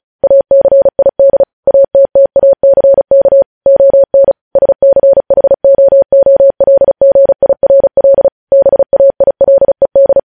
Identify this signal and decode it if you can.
Morse code, heard as a keyed tone, about 36 wpm, 550 Hz. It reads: ACID ATTACK ON SCHOOLGIRL BAILED